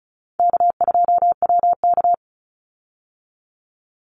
Morse code, heard as a keyed tone, about 34 wpm, 700 Hz. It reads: K2WK
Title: Call Signs Repeated 3x - 35wpm